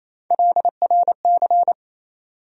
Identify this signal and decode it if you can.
Morse code, heard as a keyed tone, about 27 wpm, 700 Hz. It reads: LRC
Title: USA Call Sign Postfixes - 3 Letters 28wpm